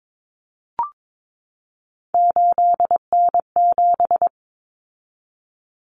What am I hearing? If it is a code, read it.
Morse code, heard as a keyed tone, about 22 wpm, 700 Hz. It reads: 8N7
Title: Common Cut Numbers - 3-Digit 22wpm